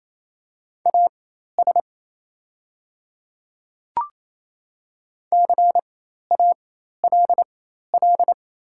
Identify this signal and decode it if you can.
Morse code, heard as a keyed tone, about 28 wpm, spaced as for 13 wpm, 700 Hz. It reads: AS CALL
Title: 4x Character Spacing - Top 300 Words - 28wpm